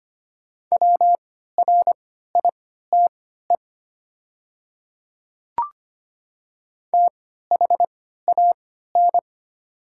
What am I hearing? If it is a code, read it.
Morse code, heard as a keyed tone, about 25 wpm, spaced as for 14 wpm, 700 Hz. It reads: WRITE THAN
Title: 3x Character Spacing - Top 100 Words - 25wpm